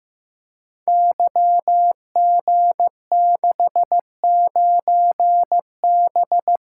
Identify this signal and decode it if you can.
Morse code, heard as a keyed tone, about 15 wpm, 700 Hz. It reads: YG69B
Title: Code Groups - 5 Letter:Numbers - 15wpm